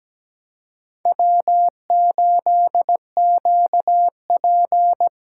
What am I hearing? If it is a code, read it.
Morse code, heard as a keyed tone, about 17 wpm, 700 Hz. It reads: W8QP
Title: USA Call Signs - 4 Characters - 8x Character Spacing - 17wpm